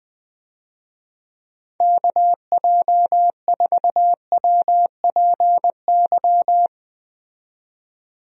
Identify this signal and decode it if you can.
Morse code, heard as a keyed tone, about 20 wpm, 700 Hz. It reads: KJ4WPY
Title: USA Call Signs - 6 Characters - 20wpm